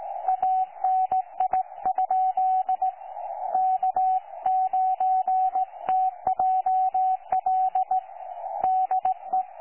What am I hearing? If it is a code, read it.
Morse code, heard as a keyed tone, about 18 wpm, 750 Hz, with radio noise. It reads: ANI? K9TJL DE
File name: 12 39Z 18088.2kHz utah aa7fv k9tjl